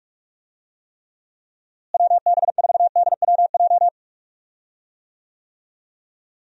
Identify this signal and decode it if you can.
Morse code, heard as a keyed tone, about 41 wpm, 700 Hz. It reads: WB4DWJ
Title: USA Call Signs - 6 Characters - Repeated 2x - 45wpm